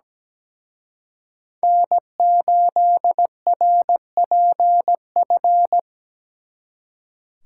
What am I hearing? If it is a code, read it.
Morse code, heard as a keyed tone, about 20 wpm, 700 Hz. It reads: N8RPF
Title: Speed Racing - USA 1x3 Call Signs 17wpm